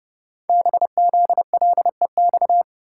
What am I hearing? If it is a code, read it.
Morse code, heard as a keyed tone, about 30 wpm, 690 Hz. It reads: BZLEX
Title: Code Groups - 5 Letters - 30wpm